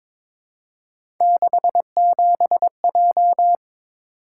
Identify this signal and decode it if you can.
Morse code, heard as a keyed tone, about 22 wpm, 700 Hz. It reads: 67J
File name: practice / 22wpm / Three Letter-Number Combinations - 22wpm